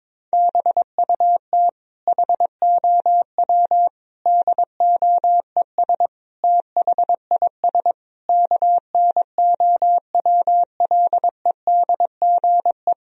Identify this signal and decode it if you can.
Morse code, heard as a keyed tone, about 22 wpm, 700 Hz. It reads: BUT HOW DOES THIS KNOWLEDGE
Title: Sentences from Top 1000 Words - Review Part 3 - 22wpm